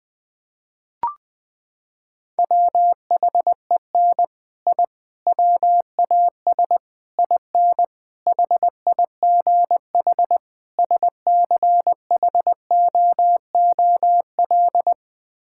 Morse code, heard as a keyed tone, about 20 wpm, 700 Hz. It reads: WHEN I WAS IN HIGH SCHOOL
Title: Sets of 6 Words - 20wpm